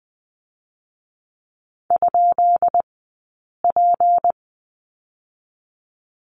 Morse code, heard as a keyed tone, about 19 wpm, 700 Hz. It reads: ? P
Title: 151 Lesson 25d - Letter p - Two Characters 20wpm